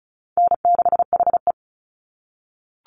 Morse code, heard as a keyed tone, about 34 wpm, 700 Hz. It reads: N6HE
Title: Call Signs - 35wpm